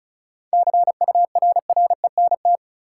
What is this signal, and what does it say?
Morse code, heard as a keyed tone, about 36 wpm, 700 Hz. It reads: CURRENT
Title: Sentences from Top 1000 Words - Review Part 4 - New Word Drill - 35wpm